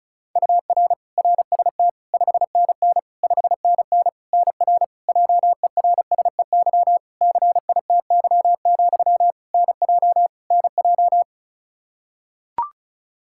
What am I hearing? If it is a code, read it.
Morse code, heard as a keyed tone, about 34 wpm, 700 Hz. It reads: UR RST 5NN 5NN NR JERSEY CITY, NJ NJ
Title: Practice QSO - Exchange 1 - Beginning - 35wpm